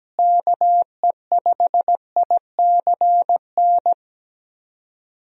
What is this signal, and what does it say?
Morse code, heard as a keyed tone, about 17 wpm, 700 Hz. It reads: KE5ICN
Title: USA Call Signs - 6 Characters - Repeated 2x - 17wpm